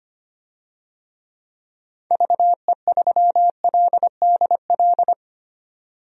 Morse code, heard as a keyed tone, about 25 wpm, 700 Hz. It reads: VE3LDL
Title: DXCC-100 Call Signs - 6 Characters - 25wpm